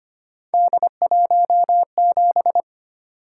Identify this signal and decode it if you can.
Morse code, heard as a keyed tone, about 25 wpm, 700 Hz. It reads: D17
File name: Three Letter-Number Combinations - 25wpm